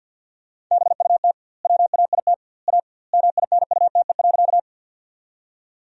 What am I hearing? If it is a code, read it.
Morse code, heard as a keyed tone, about 50 wpm, 700 Hz. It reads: BUT WAIT A MINUTE.
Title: 2x Word-Spacing - Sentences from Top 500 Words - 50wpm